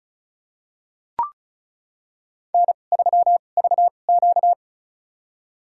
Morse code, heard as a keyed tone, about 36 wpm, spaced as for 26 wpm, 700 Hz. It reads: N3VQ
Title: USA Call Signs - 4 Characters - 2x Character Spacing - 35wpm